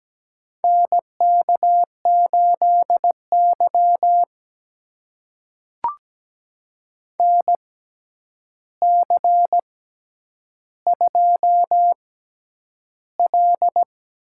Morse code, heard as a keyed tone, about 17 wpm, spaced as for 9 wpm, 700 Hz. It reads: NK8Y NC2L
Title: USA Call Signs - 4 Characters - 6x Character Spacing - 17wpm